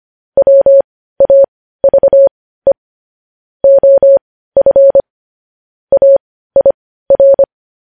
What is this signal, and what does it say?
Morse code, heard as a keyed tone, about 25 wpm, spaced as for 15 wpm, 550 Hz. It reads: WAVE OF AIR
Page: news headlines in morse code at 15 wpm